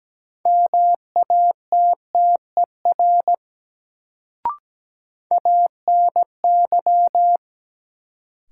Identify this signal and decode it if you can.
Morse code, heard as a keyed tone, about 17 wpm, 700 Hz. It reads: MATTER ANY
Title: Sets of 3 Words - New Word Drill - 17wpm